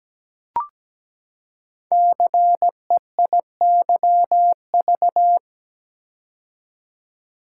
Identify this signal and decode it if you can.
Morse code, heard as a keyed tone, about 17 wpm, 700 Hz. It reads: CEIYV
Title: Code Groups - 5 Letter:Numbers - 17wpm